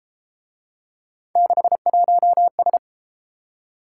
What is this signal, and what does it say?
Morse code, heard as a keyed tone, about 31 wpm, 700 Hz. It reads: 61S